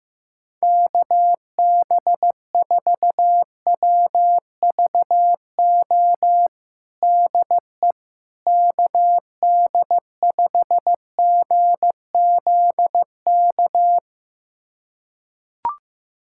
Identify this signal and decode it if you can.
Morse code, heard as a keyed tone, about 15 wpm, 700 Hz. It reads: KB4WVO DE KD5GZK
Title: USA Callsign DE Callsign 15wpm